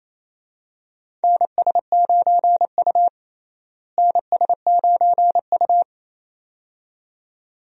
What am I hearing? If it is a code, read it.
Morse code, heard as a keyed tone, about 28 wpm, 700 Hz. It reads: NS9U NS9U
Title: USA Call Signs - 4 Characters - Repeated 2x - 28wpm